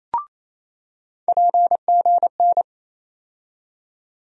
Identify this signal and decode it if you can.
Morse code, heard as a keyed tone, about 28 wpm, 700 Hz. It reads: PGN
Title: USA Call Sign Postfixes - 3 Letters 28wpm